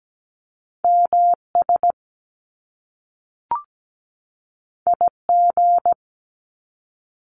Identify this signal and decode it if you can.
Morse code, heard as a keyed tone, about 17 wpm, 700 Hz. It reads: MS IG
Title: Top Two Letter Combinations in Words 17wpm